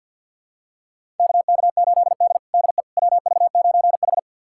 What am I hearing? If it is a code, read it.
Morse code, heard as a keyed tone, about 51 wpm, 700 Hz. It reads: KK8D DE WV9H
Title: USA Callsign DE Callsign - 4 Characters - 50wpm